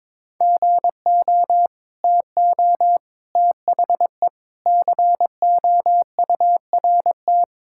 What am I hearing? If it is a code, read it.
Morse code, heard as a keyed tone, about 22 wpm, 700 Hz. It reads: GO TO THE COURT
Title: Sentences from Top 1600 Words 22wpm